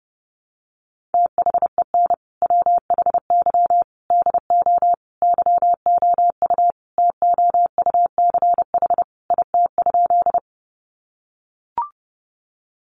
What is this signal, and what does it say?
Morse code, heard as a keyed tone, about 30 wpm, 700 Hz. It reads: THEN WHY DO YOU TOUCH IT?
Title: Sentences from Top 1200 Words 30wpm